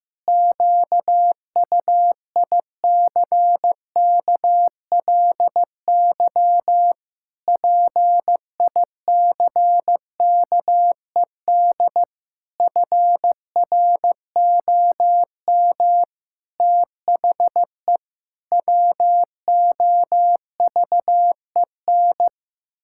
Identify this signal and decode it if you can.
Morse code, heard as a keyed tone, about 16 wpm, 700 Hz. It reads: QUICKLY PICKED FROM THE WOVEN
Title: Pangram Challenge v2 - 15wpm